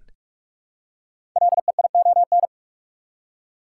Morse code, heard as a keyed tone, about 44 wpm, 700 Hz. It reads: LEION
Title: Code Groups - 5 Letter:Numbers - 45wpm